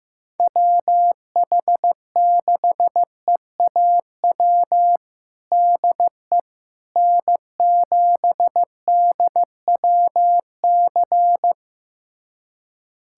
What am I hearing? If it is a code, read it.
Morse code, heard as a keyed tone, about 16 wpm, 700 Hz. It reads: WH6EAW DE N7DWC
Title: USA Callsign DE Callsign 15wpm